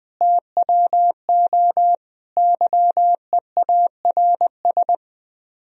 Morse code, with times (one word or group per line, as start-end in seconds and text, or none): 0.2-1.9 TWO
2.4-4.9 YEARS